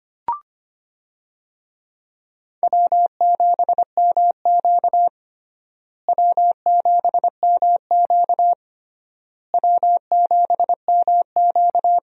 Morse code, heard as a keyed tone, about 25 wpm, 700 Hz. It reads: W7MQ W7MQ W7MQ
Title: USA Call Signs - 4 Characters - Repeated 3x - 25wpm